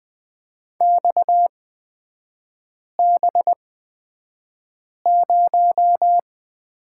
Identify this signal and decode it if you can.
Morse code, heard as a keyed tone, about 20 wpm, spaced as for 5 wpm, 700 Hz. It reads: XB0
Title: 250 Lesson 40e - BK - Three Characters 20wpm